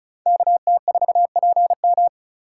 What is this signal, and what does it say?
Morse code, heard as a keyed tone, about 36 wpm, 700 Hz. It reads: KT4PM
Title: USA Call Signs - 2x Character Spacing - 35wpm